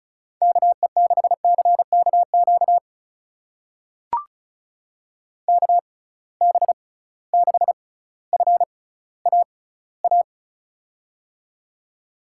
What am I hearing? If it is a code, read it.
Morse code, heard as a keyed tone, about 36 wpm, spaced as for 18 wpm, 700 Hz. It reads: KE6CKQ KB6FAA
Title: USA Call Signs - 6 Characters - 6x Character Spacing - 35wpm